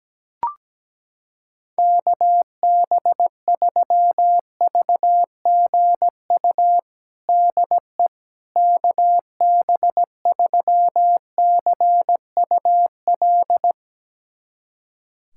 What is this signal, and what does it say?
Morse code, heard as a keyed tone, about 17 wpm, 700 Hz. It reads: KB3VGU DE KB3CUL
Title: USA Callsign DE Callsign 17wpm